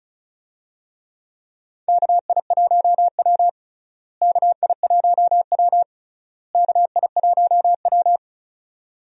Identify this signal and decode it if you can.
Morse code, heard as a keyed tone, about 36 wpm, 700 Hz. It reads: KI1W KI1W KI1W
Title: USA Call Signs - 4 Characters - Repeated 3x - 35wpm